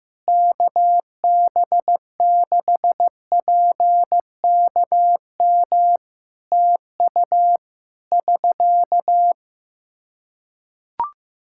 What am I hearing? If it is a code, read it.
Morse code, heard as a keyed tone, about 16 wpm, 700 Hz. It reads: KB6PKM TU <SK>
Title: Practice QSO - Exchange 4 - Ending - 15wpm